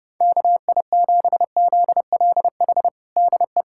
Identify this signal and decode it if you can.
Morse code, heard as a keyed tone, about 32 wpm, 700 Hz. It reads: KI7ZLH DE
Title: USA Callsign DE Callsign - 6 Characters - 30wpm